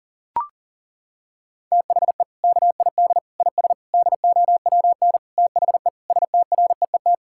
Morse code, heard as a keyed tone, about 37 wpm, 700 Hz. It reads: THE KID IS DOWN THE STREET
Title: Sentences from Top 1600 Words 40wpm